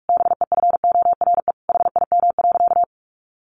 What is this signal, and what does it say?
Morse code, heard as a keyed tone, about 44 wpm, 700 Hz. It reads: BEFORE HIM.
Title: Sentences from Top 1400 Words 45wpm